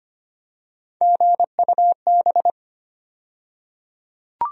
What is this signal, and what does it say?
Morse code, heard as a keyed tone, about 25 wpm, 700 Hz. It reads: GUB
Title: USA Call Sign Postfixes - 3 Letters 25wpm